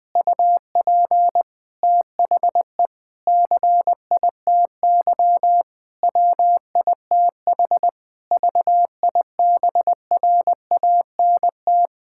Morse code, heard as a keyed tone, about 21 wpm, 700 Hz. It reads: UP THE CITY WITH VIBRANT
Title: Difficult Sentences with Hints - 20wpm